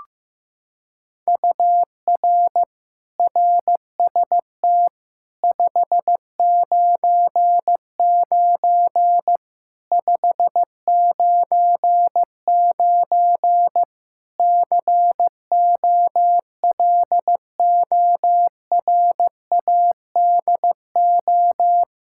Morse code, heard as a keyed tone, about 16 wpm, 700 Hz. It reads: UR RST 599 599 COLORADO
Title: K1USN SST QSO Practice 15wpm